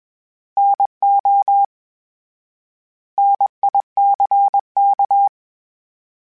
Morse code, heard as a keyed tone, about 21 wpm, 800 Hz. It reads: NO NICK
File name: Misc / Misc-mp3s / Misc @20